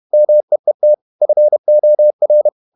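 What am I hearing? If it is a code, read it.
Morse code, heard as a keyed tone, about 31 wpm, 600 Hz. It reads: MEET FOR
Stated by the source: Morse code at 30 wpm, the novel Never by Ken Follett in English, from CW med Gnister.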